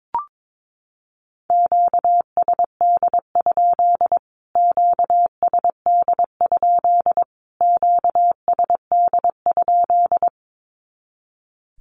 Morse code, heard as a keyed tone, about 22 wpm, 700 Hz. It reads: QSD? QSD? QSD?
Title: Q-Codes Repeated 3x 22wpm